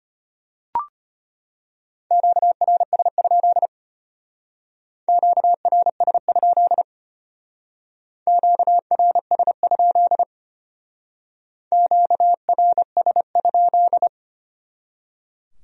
Morse code, heard as a keyed tone, about 32 wpm, 700 Hz. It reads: QRS? QRS? QRS? QRS?